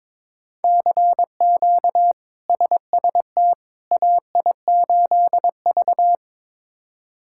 Morse code, heard as a keyed tone, about 22 wpm, 700 Hz. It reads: CQ SST AI8V
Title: K1USN SST QSO Practice - 22wpm